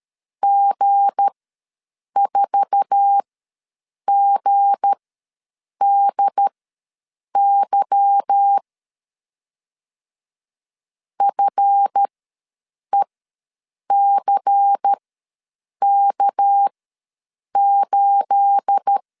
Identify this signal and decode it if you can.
Morse code, heard as a keyed tone, about 13 wpm, spaced as for 7 wpm, 800 Hz. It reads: G4GDY FECK8
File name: LEZIONI / 9 286 ws 12wpm 3351